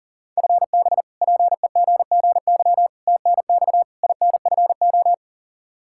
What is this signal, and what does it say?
Morse code, heard as a keyed tone, about 40 wpm, 700 Hz. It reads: FB PEGGY TNX INFO
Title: Practice QSO - Exchange 3 - More Info - 40wpm